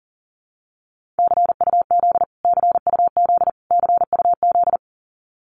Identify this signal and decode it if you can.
Morse code, heard as a keyed tone, about 40 wpm, 700 Hz. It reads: CUZ CUZ CUZ
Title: Abbreviations Repeated 3x - 40wpm